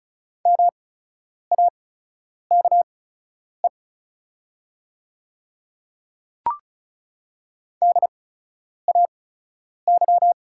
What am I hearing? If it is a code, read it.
Morse code, heard as a keyed tone, about 36 wpm, spaced as for 10 wpm, 700 Hz. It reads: MAKE DAY
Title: 8x Character Spacing - Top 100 Words - 35wpm